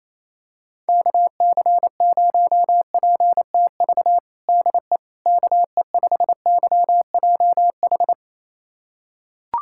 Morse code, heard as a keyed tone, about 28 wpm, 700 Hz. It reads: KC0PTV DE KE5YJH
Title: USA Callsign DE Callsign 28wpm